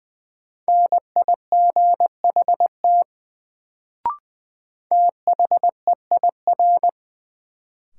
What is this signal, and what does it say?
Morse code, heard as a keyed tone, about 20 wpm, 700 Hz. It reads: NIGHT THEIR
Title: Sentences from Top 300 Words - New Word Drill - 20wpm